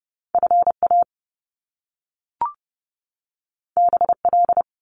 Morse code, heard as a keyed tone, about 30 wpm, 700 Hz. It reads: FA BL
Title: Top Two Letter Combinations in Words 30wpm